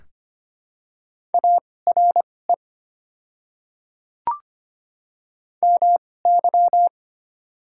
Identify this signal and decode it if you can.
Morse code, heard as a keyed tone, about 25 wpm, spaced as for 18 wpm, 690 Hz. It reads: ARE MY
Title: 2x Character Spacing - Top 100 Words - 25wpm